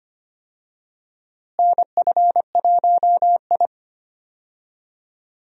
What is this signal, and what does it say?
Morse code, heard as a keyed tone, about 25 wpm, 700 Hz. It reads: NF1I
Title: USA Call Signs - 4 Characters - Repeated 2x - 25wpm